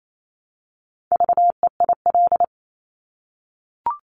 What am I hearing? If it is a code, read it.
Morse code, heard as a keyed tone, about 28 wpm, 700 Hz. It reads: VEIL
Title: Multi-letter Phonemes Repeated 3x 28wpm